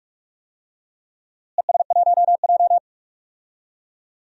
Morse code, heard as a keyed tone, about 43 wpm, 700 Hz. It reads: ES1J
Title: DXCC-100 Call Signs - 4 Characters - Repeated 3x - 45wpm